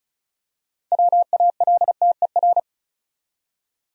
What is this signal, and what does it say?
Morse code, heard as a keyed tone, about 34 wpm, 700 Hz. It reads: WALTER
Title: USA Common Names - 35wpm